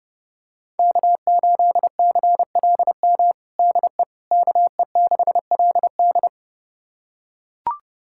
Morse code, heard as a keyed tone, about 32 wpm, 700 Hz. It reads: K8CLM DE KE6LD
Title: USA Callsign DE Callsign - 5 Characters - 30wpm